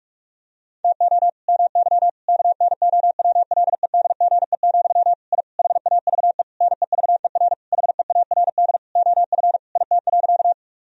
Morse code, heard as a keyed tone, about 41 wpm, 700 Hz. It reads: TO MY KNOWLEDGE, I HAVE NEVER HEARD OF IT.
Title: Sentences from Top 1000 Words - Review Part 3 - 45wpm